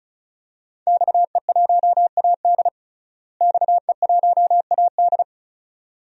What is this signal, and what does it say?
Morse code, heard as a keyed tone, about 34 wpm, 700 Hz. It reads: XE1AD XE1AD
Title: DXCC-100 Call Signs - 5 Characters - Repeated 2x - 35wpm